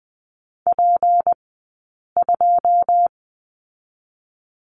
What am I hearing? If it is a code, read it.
Morse code, heard as a keyed tone, about 19 wpm, spaced as for 8 wpm, 700 Hz. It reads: P2
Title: 211 Lesson 34d - Number 8 - Two Characters 20wpm